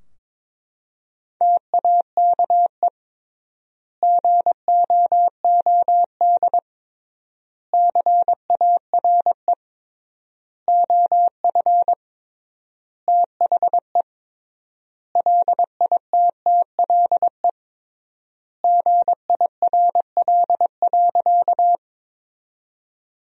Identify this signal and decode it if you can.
Morse code, heard as a keyed tone, about 22 wpm, 700 Hz. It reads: TAKE GOOD CARE OF THE LITTLE GIRL.
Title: 3x Word-Spacing - Sentences from Top 400 Words - 22wpm